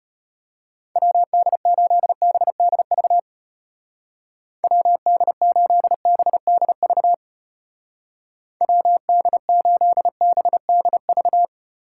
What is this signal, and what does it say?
Morse code, heard as a keyed tone, about 31 wpm, 700 Hz. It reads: WD8BDV WD8BDV WD8BDV
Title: Speed Racing - USA 2x3 Call Signs 25wpm